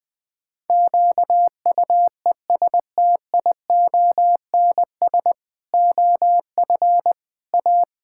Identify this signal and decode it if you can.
Morse code, heard as a keyed tone, about 20 wpm, 700 Hz. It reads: QUESTIONS OF A
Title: Pangram Challenge v2 - 20wpm